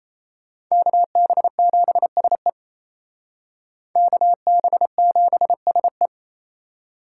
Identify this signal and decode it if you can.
Morse code, heard as a keyed tone, about 36 wpm, 700 Hz. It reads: KB7SE KB7SE